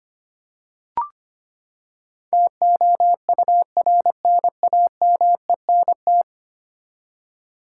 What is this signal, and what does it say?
Morse code, heard as a keyed tone, about 25 wpm, 700 Hz. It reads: TOURNAMENT
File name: Ten Letter Words - 25wpm